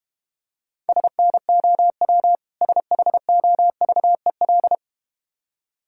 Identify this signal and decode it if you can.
Morse code, heard as a keyed tone, about 32 wpm, 700 Hz. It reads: SNOW SHOVEL